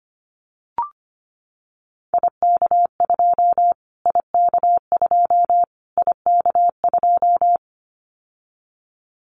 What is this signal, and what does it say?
Morse code, heard as a keyed tone, about 25 wpm, 700 Hz. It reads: IK2 IK2 IK2
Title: Callsign Prefixes for Top 25 DXCC - Repeated 3x - 25wpm